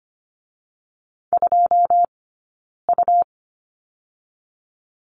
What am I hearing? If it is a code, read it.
Morse code, heard as a keyed tone, about 25 wpm, 700 Hz. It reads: 2 U
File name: course / 25:10wpm / 099 Lesson 17d - Letter u - Two Characters 25wpm